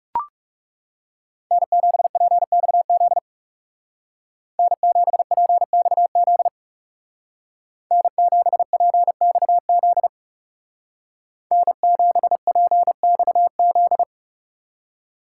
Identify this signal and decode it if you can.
Morse code, heard as a keyed tone, about 39 wpm, 700 Hz. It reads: N7PXZ N7PXZ N7PXZ N7PXZ